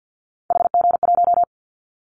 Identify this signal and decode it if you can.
Morse code, heard as a keyed tone, about 49 wpm, 700 Hz. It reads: HG1
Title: Callsign Prefixes for Top 25 DXCC - Repeated 3x - 50wpm